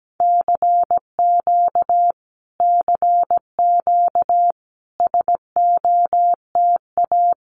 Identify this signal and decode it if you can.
Morse code, heard as a keyed tone, about 17 wpm, 700 Hz. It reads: CQ CQ SOTA
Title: Common QSO Phrases Encore 17wpm